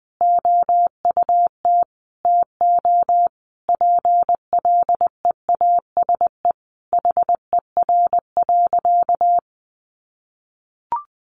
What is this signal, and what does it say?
Morse code, heard as a keyed tone, about 20 wpm, 700 Hz. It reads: OUT TO PLEASE HER.
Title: Sentences from Top 1500 Words 20wpm